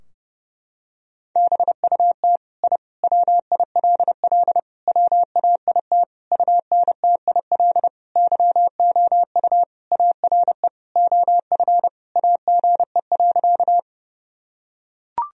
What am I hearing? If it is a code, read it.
Morse code, heard as a keyed tone, about 30 wpm, 700 Hz. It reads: BUT I WILL WAIT UNTIL YOU ARE OF AGE.
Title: Sentences from Top 1000 Words - Review Part 3 - 30wpm